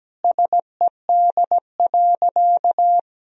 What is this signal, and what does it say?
Morse code, heard as a keyed tone, about 17 wpm, 700 Hz. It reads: SED.
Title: Sentences from Top 100 Words and Abbreviations 17wpm